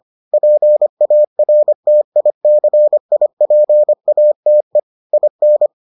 Morse code, heard as a keyed tone, about 25 wpm, 600 Hz. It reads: PARTICIPATE IN